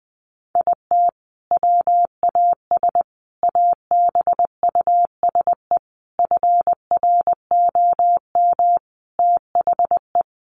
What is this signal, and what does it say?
Morse code, heard as a keyed tone, about 20 wpm, 700 Hz. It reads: IT WAS ABUSE FROM THE
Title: Sentences from Top 1300 Words 20wpm